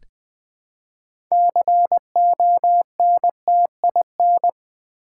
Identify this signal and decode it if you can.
Morse code, heard as a keyed tone, about 20 wpm, 700 Hz. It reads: CONTIN
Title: Build A Word - 3 Letters At A Time - 9-Letter Words - 20wpm